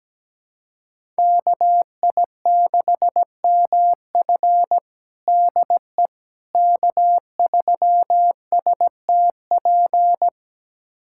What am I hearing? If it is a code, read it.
Morse code, heard as a keyed tone, about 17 wpm, 700 Hz. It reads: KI6MF DE K3STP
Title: USA Callsign DE Callsign - 5 Characters - 17wpm